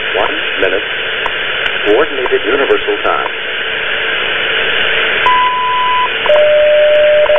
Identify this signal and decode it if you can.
Radio time signal, with speech.
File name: WWV time signal